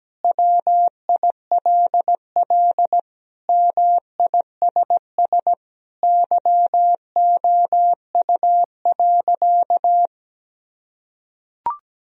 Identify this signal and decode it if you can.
Morse code, heard as a keyed tone, about 17 wpm, 700 Hz. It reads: WILL MISS YOU.
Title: Sentences from Top 1000 Words - Review Part 3 - 17wpm